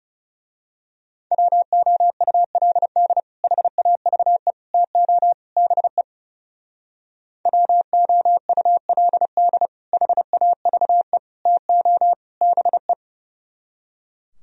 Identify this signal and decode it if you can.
Morse code, heard as a keyed tone, about 34 wpm, 700 Hz. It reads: WOULD HAVE TO BE WOULD HAVE TO BE
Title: Speed-Racing - Sets of 4 Words 30wpm